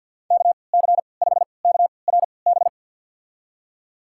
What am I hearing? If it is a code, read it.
Morse code, heard as a keyed tone, about 50 wpm, spaced as for 28 wpm, 700 Hz. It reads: KC5KRB
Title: USA Call Signs - 3x Character Spacing - 50wpm